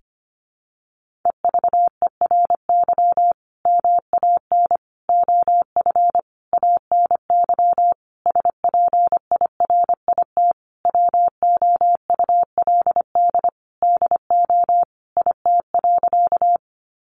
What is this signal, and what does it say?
Morse code, heard as a keyed tone, about 25 wpm, 700 Hz. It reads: EVERY MAN OF ANY SPIRIT WOULD DO IT.
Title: Sentences from Top 1400 Words 25wpm